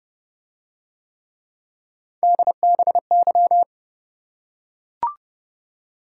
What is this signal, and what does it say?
Morse code, heard as a keyed tone, about 29 wpm, 700 Hz. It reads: DBY